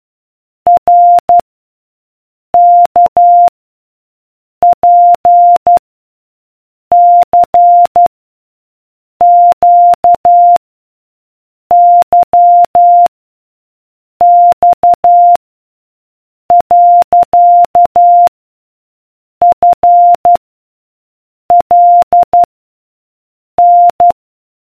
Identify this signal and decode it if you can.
Morse code, heard as a keyed tone, about 12 wpm, 700 Hz. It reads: R K P C Q Y X . F L N